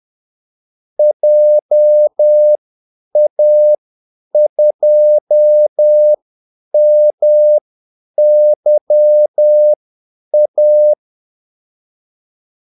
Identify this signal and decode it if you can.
Morse code, heard as a keyed tone, about 10 wpm, spaced as for 8 wpm, 600 Hz. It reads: JA2MYA
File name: Callsigns-10-8wpm